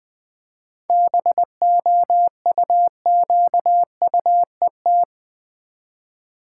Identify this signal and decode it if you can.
Morse code, heard as a keyed tone, about 20 wpm, 700 Hz. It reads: BOUQUET
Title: Easy Words with Hints - 20wpm